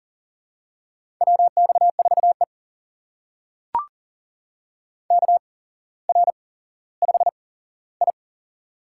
Morse code, heard as a keyed tone, about 40 wpm, spaced as for 17 wpm, 700 Hz. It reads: WX4E KR5I
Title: USA Call Signs - 4 Characters - 8x Character Spacing - 40wpm